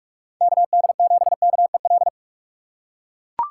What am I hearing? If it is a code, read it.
Morse code, heard as a keyed tone, about 41 wpm, 700 Hz. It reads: KD7KEL
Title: USA Call Signs - 6x Character Spacing - 45wpm